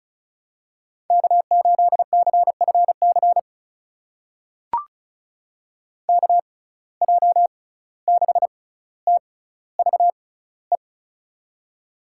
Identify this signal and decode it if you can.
Morse code, heard as a keyed tone, about 36 wpm, spaced as for 17 wpm, 700 Hz. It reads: K8CFC KJ6TVE
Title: USA Call Signs - 6x Character Spacing - 35wpm